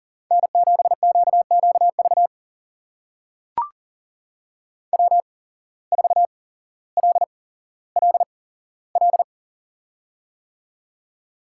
Morse code, heard as a keyed tone, about 42 wpm, spaced as for 17 wpm, 700 Hz. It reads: N7QQV W4LLL
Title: USA Call Signs - 5 Characters - 8x Character Spacing - 40wpm